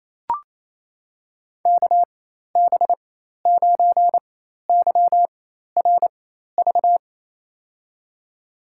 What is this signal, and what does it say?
Morse code, heard as a keyed tone, about 28 wpm, spaced as for 13 wpm, 700 Hz. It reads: KB9YRV
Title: USA Call Signs - 6 Characters - 4x Character Spacing - 28wpm